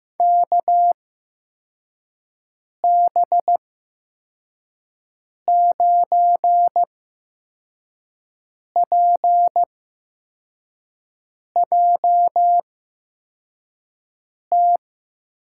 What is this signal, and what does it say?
Morse code, heard as a keyed tone, about 15 wpm, spaced as for 4 wpm, 700 Hz. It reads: KB9PJT
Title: USA Call Signs - 6 Characters - 8x Character Spacing - 15wpm